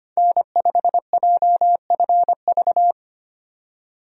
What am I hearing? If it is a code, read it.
Morse code, heard as a keyed tone, about 25 wpm, 700 Hz. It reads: N5JFV
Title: USA Call Signs - 5 Characters - 25wpm